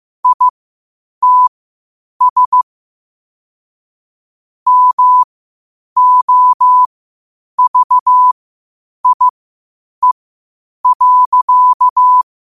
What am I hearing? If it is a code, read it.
Morse code, heard as a keyed tone, about 15 wpm, spaced as for 8 wpm, 1000 Hz. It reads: ITS MOVIE.
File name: qotdfiles / 8wpm